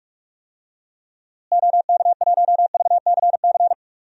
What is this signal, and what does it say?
Morse code, heard as a keyed tone, about 46 wpm, 700 Hz. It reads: OK1VCC
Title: DXCC-100 Call Signs - 6 Characters - Repeated 2x - 45wpm